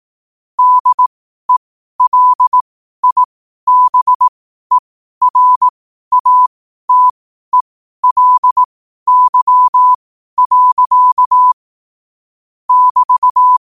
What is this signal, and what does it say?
Morse code, heard as a keyed tone, about 18 wpm, spaced as for 12 wpm, 1000 Hz. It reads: DELIBERATELY. <BT>
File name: qotdfiles / 12wpm